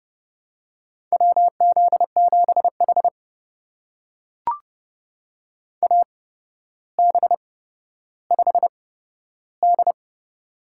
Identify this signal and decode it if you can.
Morse code, heard as a keyed tone, about 28 wpm, spaced as for 13 wpm, 700 Hz. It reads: WZ7H AB5D
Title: USA Call Signs - 4 Characters - 8x Character Spacing - 30wpm